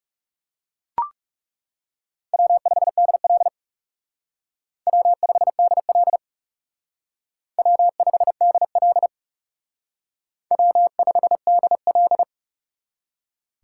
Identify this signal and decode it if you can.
Morse code, heard as a keyed tone, about 37 wpm, 700 Hz. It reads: W5DL W5DL W5DL W5DL
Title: Speed Racing - USA 1x2 Call Signs 30wpm